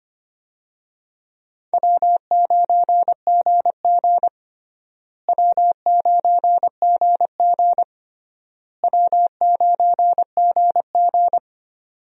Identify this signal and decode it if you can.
Morse code, heard as a keyed tone, about 25 wpm, 700 Hz. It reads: W9GG W9GG W9GG
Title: USA Call Signs - 4 Characters - Repeated 3x - 25wpm